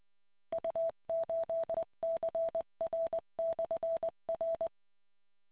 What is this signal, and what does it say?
Morse code, heard as a keyed tone, about 20 wpm, 660 Hz. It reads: U8CR/R